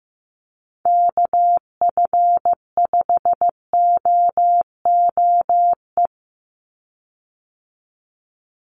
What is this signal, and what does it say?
Morse code, heard as a keyed tone, about 15 wpm, 700 Hz. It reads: KF5OOE
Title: USA Call Signs 15wpm